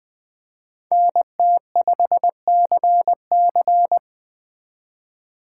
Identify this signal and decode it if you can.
Morse code, heard as a keyed tone, about 20 wpm, 700 Hz. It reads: NT5CC
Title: Speed Racing - USA 2x2 Call Signs 17wpm